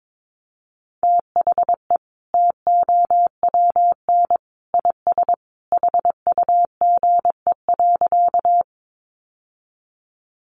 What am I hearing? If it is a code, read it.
Morse code, heard as a keyed tone, about 22 wpm, 700 Hz. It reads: THE TOWN IS HUGE.
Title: Sentences from Top 1100 Words 22wpm